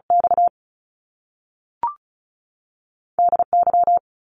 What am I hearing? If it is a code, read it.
Morse code, heard as a keyed tone, about 36 wpm, spaced as for 6 wpm, 700 Hz. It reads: XDY
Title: Top Two Letter Combinations in Words 35wpm